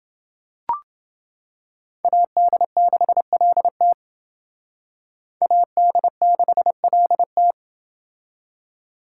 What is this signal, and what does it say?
Morse code, heard as a keyed tone, about 29 wpm, 700 Hz. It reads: AD6LT AD6LT